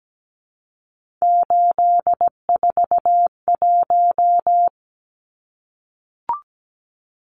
Morse code, heard as a keyed tone, about 17 wpm, 700 Hz. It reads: 841
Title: Numbers - 3-Digit 17wpm